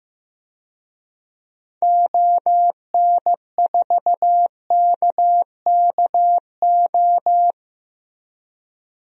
Morse code, heard as a keyed tone, about 16 wpm, 700 Hz. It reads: ON4KKO
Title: DXCC-100 Call Signs - 6 Characters - 15wpm